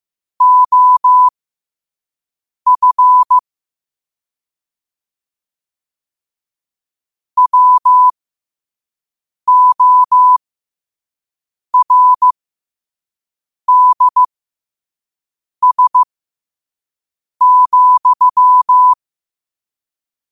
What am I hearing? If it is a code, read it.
Morse code, heard as a keyed tone, about 15 wpm, spaced as for 5 wpm, 1000 Hz. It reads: OF WORDS,
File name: qotdfiles / 5wpm